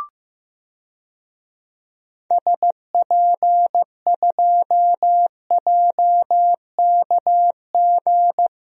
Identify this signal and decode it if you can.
Morse code, heard as a keyed tone, about 15 wpm, 690 Hz. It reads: SP2JKG
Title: DXCC-100 Call Signs - 6 Characters - 15wpm